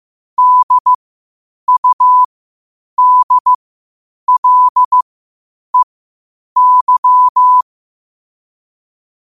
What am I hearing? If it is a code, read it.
Morse code, heard as a keyed tone, about 15 wpm, spaced as for 8 wpm, 1000 Hz. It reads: DUDLEY
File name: qotdfiles / 8wpm